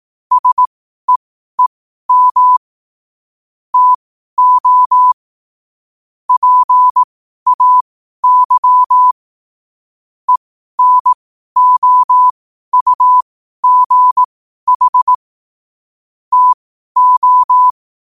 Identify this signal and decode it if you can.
Morse code, heard as a keyed tone, about 18 wpm, spaced as for 12 wpm, 1000 Hz. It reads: SEEM TO PAY ENOUGH TO
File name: qotdfiles / 12wpm